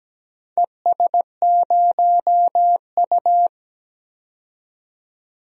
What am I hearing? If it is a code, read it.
Morse code, heard as a keyed tone, about 17 wpm, 700 Hz. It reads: ES0U
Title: DXCC-100 Call Signs - 4 Characters - 17wpm